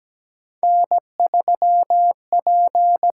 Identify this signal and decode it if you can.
Morse code, heard as a keyed tone, about 17 wpm, 700 Hz. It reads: N3P
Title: USA Call Signs - 5 Characters - 8x Character Spacing - 17wpm